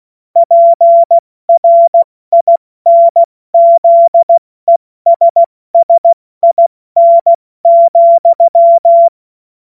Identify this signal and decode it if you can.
Morse code, heard as a keyed tone, about 16 wpm, 680 Hz. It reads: PRINZESSIN,
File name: Morsekurs / Witze / Witz 05. April 2022 70 F80